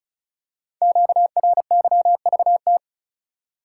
Morse code, heard as a keyed tone, about 36 wpm, 700 Hz. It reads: QRYVT
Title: Code Groups - 5 Letter:Numbers - 35wpm